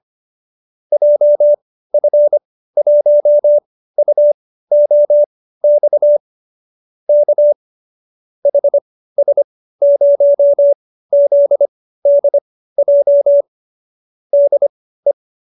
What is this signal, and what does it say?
Morse code, heard as a keyed tone, about 25 wpm, spaced as for 15 wpm, 570 Hz. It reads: JF1UOX K HS0ZDJ DE